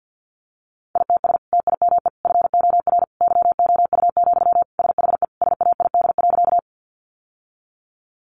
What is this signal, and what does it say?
Morse code, heard as a keyed tone, about 48 wpm, 700 Hz. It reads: ITS TIME FOR YOU, SHE SAID.
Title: Sentences from Top 600 Words 50wpm